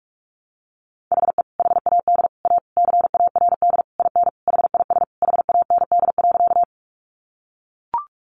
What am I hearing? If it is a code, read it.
Morse code, heard as a keyed tone, about 44 wpm, 700 Hz. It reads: HE HAD A CARD IN HIS HAND.
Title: Sentences from Top 600 Words 45wpm